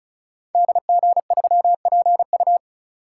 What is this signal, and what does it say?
Morse code, heard as a keyed tone, about 36 wpm, 700 Hz. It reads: DG3PU
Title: Code Groups - 5 Letter:Numbers - 35wpm